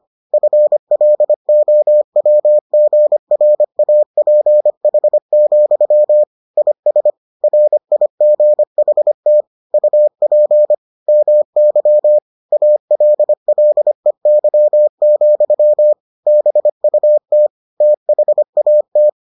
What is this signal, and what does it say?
Morse code, heard as a keyed tone, about 25 wpm, 600 Hz. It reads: FLOWGRAPH, IS RIGHT UP MY ALLEY, BUT THAT